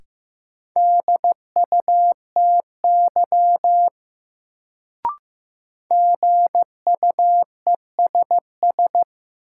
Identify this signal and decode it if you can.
Morse code, heard as a keyed tone, about 15 wpm, 700 Hz. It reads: DUTY GUESS
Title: Sentences from Top 1500 Words - New Word Drill Alt - 15wpm